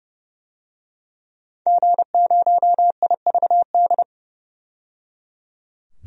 Morse code, heard as a keyed tone, about 30 wpm, 700 Hz. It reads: G0IVD
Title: DXCC-100 Call Signs - 5 Characters - 30wpm